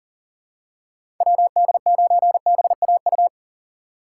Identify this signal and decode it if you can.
Morse code, heard as a keyed tone, about 40 wpm, 700 Hz. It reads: WD9BAU
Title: USA Call Signs - 6 Characters - 3x Character Spacing - 40wpm